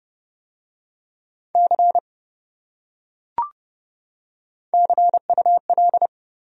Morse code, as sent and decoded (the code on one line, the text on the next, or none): -.-. / -.-. ..- .-..
C CUL